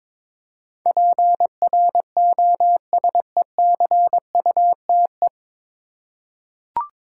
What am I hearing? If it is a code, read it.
Morse code, heard as a keyed tone, about 22 wpm, 700 Hz. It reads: PROSECUTE
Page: Morse Code Ninja practice 22wpm